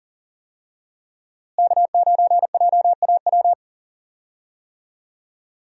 Morse code, heard as a keyed tone, about 40 wpm, 700 Hz. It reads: K9JAW
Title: USA Call Signs - 5 Characters - Repeated 3x - 40wpm